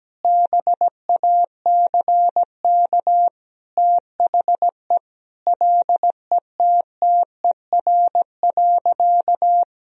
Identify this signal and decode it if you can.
Morse code, heard as a keyed tone, about 17 wpm, 700 Hz. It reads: BACK THE LETTER.
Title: Sentences from Top 1000 Words - Review Part 4 - 17wpm